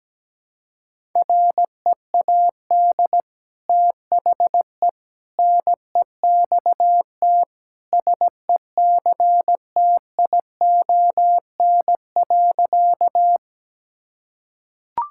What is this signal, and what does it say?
Morse code, heard as a keyed tone, about 17 wpm, 700 Hz. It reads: READ THE NEXT SECTION.
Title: Sentences from Top 1000 Words - Review Part 5 - 17wpm